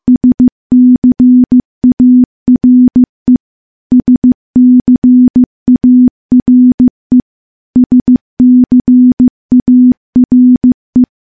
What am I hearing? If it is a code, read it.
Morse code, heard as a keyed tone, about 14 wpm, 250 Hz. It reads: SCARE SCARE SCARE
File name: morse / morse-training-15WPM